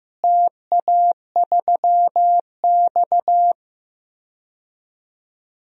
Morse code, heard as a keyed tone, about 16 wpm, 700 Hz. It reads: TA3X
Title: DXCC-100 Call Signs - 4 Characters - Repeated 3x - 15wpm